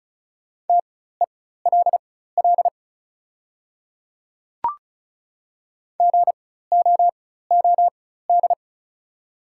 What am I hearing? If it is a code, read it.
Morse code, heard as a keyed tone, about 36 wpm, spaced as for 16 wpm, 700 Hz. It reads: TELL GOOD
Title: 4x Character Spacing - Top 200 Words - 35wpm